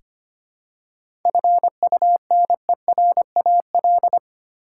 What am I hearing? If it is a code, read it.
Morse code, heard as a keyed tone, about 25 wpm, 700 Hz. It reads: FUNERAL
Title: Seven Letter Words - 25wpm